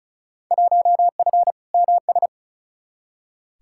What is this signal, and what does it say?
Morse code, heard as a keyed tone, about 36 wpm, 700 Hz. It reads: 1F MS
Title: Field Day Exchanges Encore - 35wpm